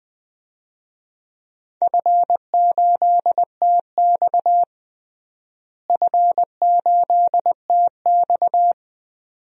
Morse code, heard as a keyed tone, about 20 wpm, 700 Hz. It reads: F8TX F8TX
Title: DXCC-100 Call Signs - 4 Characters - Repeated 2x - 20wpm